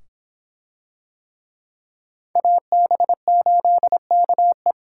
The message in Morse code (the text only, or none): AB8KE